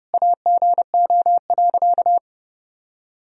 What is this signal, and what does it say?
Morse code, heard as a keyed tone, about 32 wpm, 700 Hz. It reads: AGO.
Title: Sentences from Top 1000 Words - Review Part 3 - 30wpm